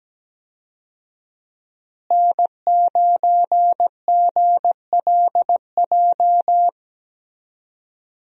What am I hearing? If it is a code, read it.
Morse code, heard as a keyed tone, about 17 wpm, 700 Hz. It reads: N9GLJ
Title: USA Call Signs - 5 Characters - Repeated 3x - 17wpm